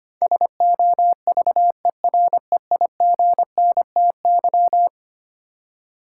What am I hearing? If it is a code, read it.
Morse code, heard as a keyed tone, about 25 wpm, 700 Hz. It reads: SOVEREIGNTY
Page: morse-code-ninja-practice-25wpm